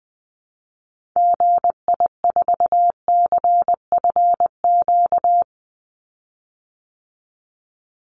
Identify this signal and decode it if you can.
Morse code, heard as a keyed tone, about 20 wpm, 700 Hz. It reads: GI4CFQ
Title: Call Signs Repeated 3x - 20wpm